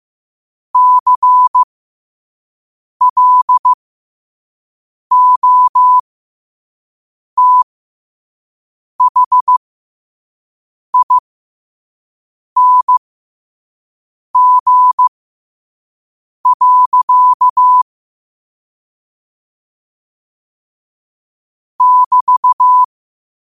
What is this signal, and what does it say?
Morse code, heard as a keyed tone, about 15 wpm, spaced as for 5 wpm, 1000 Hz. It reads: CLOTHING. <BT>
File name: qotdfiles / 5wpm